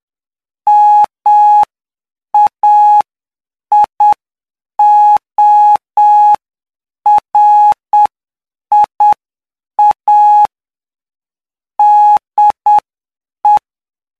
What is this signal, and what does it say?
Morse code, heard as a keyed tone, about 7 wpm, spaced as for 6 wpm, 840 Hz. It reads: MAIORIA DE
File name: cw / py2lcd / 15-PY2LCD-Li��o 24